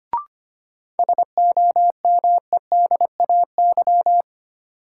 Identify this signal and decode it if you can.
Morse code, heard as a keyed tone, about 25 wpm, 700 Hz. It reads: SOMEDAY
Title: Sentences from Top 1600 Words - New Word Drill Alt - 25wpm